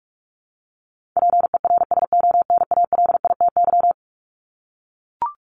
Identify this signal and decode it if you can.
Morse code, heard as a keyed tone, about 44 wpm, 700 Hz. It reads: PERSONALITY
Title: Top 1251-1500 Words 45wpm